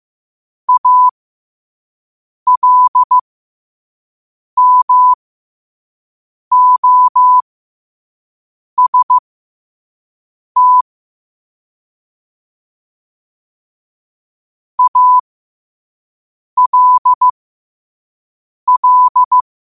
Morse code, heard as a keyed tone, about 15 wpm, spaced as for 5 wpm, 1000 Hz. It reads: ALMOST ALL
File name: qotdfiles / 5wpm